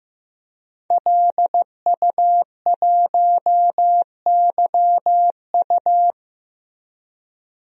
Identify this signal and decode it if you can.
Morse code, heard as a keyed tone, about 15 wpm, 700 Hz. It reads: LU1YU
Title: DXCC-100 Call Signs - 5 Characters - Repeated 2x - 15wpm